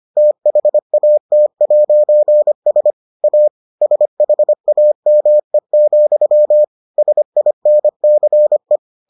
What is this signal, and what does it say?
Morse code, heard as a keyed tone, about 25 wpm, 600 Hz. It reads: THAT'S A SHAME, SINCE